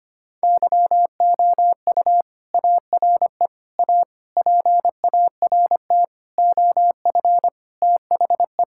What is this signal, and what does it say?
Morse code, heard as a keyed tone, about 25 wpm, 700 Hz. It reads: YOU ARE A PART OF THE
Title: Sentences from Top 1000 Words - Review Part 6 - 25wpm